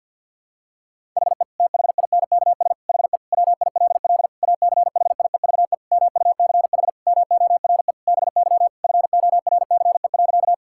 Morse code, heard as a keyed tone, about 49 wpm, 700 Hz. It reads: HE THINKS HE WILL ACHIEVE MUCH MORE BY FORCE.
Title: Sentences from Top 1000 Words - Review Part 6 - 50wpm